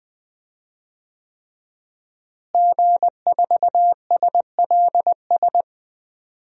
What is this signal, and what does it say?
Morse code, heard as a keyed tone, about 20 wpm, 700 Hz. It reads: G4SLS
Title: DXCC-100 Call Signs - 5 Characters - Repeated 3x - 20wpm